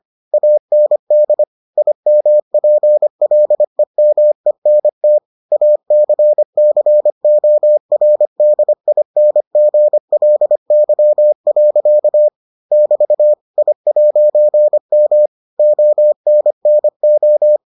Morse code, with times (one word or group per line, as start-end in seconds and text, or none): 0.3-1.4 AND
1.8-5.2 IMPLEMENT
5.5-12.3 ACCORDINGLY.
12.7-13.3 <BT>
13.6-15.3 I'M
15.6-17.6 ONNO